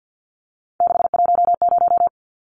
Morse code, heard as a keyed tone, about 48 wpm, 700 Hz. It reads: 610
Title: Numbers - 3-Digit 50wpm